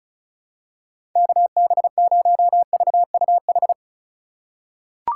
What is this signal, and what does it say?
Morse code, heard as a keyed tone, about 34 wpm, 700 Hz. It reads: KB0VUH
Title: USA Call Signs - 6 Characters - 3x Character Spacing - 35wpm